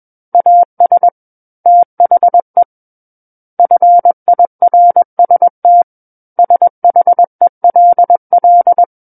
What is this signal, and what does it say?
Morse code, heard as a keyed tone, about 21 wpm, 700 Hz. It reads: AS THE FIRST SHELL
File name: Book One / 12-Chapter 12 - 21 WPM